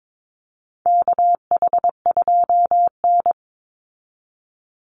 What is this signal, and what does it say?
Morse code, heard as a keyed tone, about 22 wpm, 700 Hz. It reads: KH2N
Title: Call Signs Repeated 3x - 22wpm